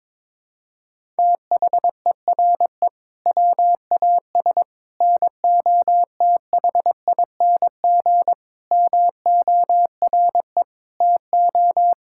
Morse code, heard as a keyed tone, about 21 wpm, 700 Hz. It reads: THERE WAS NOTHING MORE TO
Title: Sentences from Top 1000 Words - Review Part 5 - 22wpm